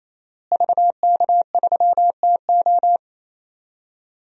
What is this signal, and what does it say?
Morse code, heard as a keyed tone, about 27 wpm, 700 Hz. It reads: VK3TO
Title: DXCC-100 Call Signs - 5 Characters - Repeated 3x - 28wpm